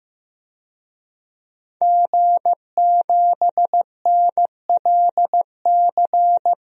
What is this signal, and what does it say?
Morse code, heard as a keyed tone, about 15 wpm, 700 Hz. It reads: G7NLC
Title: DXCC-100 Call Signs - 5 Characters - Repeated 2x - 15wpm